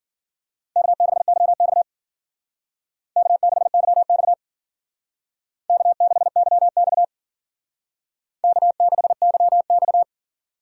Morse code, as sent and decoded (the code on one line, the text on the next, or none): -.- -.... -.-- -..- / -.- -.... -.-- -..- / -.- -.... -.-- -..- / -.- -.... -.-- -..-
K6YX K6YX K6YX K6YX